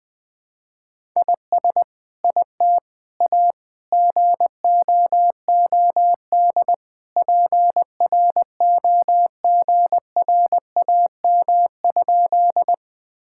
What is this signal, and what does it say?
Morse code, heard as a keyed tone, about 20 wpm, 700 Hz. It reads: IS IT A GOOD PROGRAM?
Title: Sentences from Top 1000 Words - Review Part 5 - 20wpm